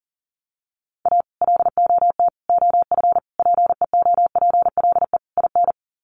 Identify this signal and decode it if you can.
Morse code, heard as a keyed tone, about 40 wpm, 700 Hz. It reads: A LOT OF PEOPLE IN
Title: Sets of 5 Words 40wpm